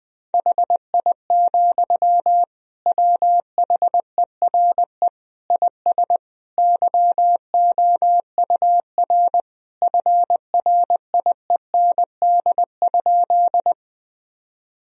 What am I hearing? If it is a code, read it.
Morse code, heard as a keyed tone, about 20 wpm, 700 Hz. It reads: HI, WHERE IS YOUR FRIEND?
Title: Sentences from Top 1000 Words - Review Part 3 - 20wpm